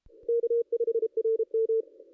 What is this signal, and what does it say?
Morse code, heard as a keyed tone, about 33 wpm, 460 Hz. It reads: K5RM
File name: contests / ss-cw-k5tr-op-n5rz -0400